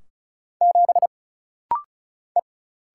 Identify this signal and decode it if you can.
Morse code, heard as a keyed tone, about 33 wpm, 700 Hz. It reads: 7 E
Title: Single Letter-Number - Rapid-Fire 35wpm